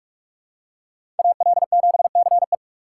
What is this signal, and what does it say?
Morse code, heard as a keyed tone, about 44 wpm, 700 Hz. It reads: AL7CE
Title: USA Call Signs - 6x Character Spacing - 45wpm